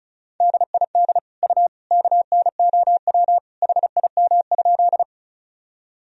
Morse code, heard as a keyed tone, about 36 wpm, 700 Hz. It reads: DID U KNOW HIM?
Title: Sentences from Top 100 Words and Abbreviations 35wpm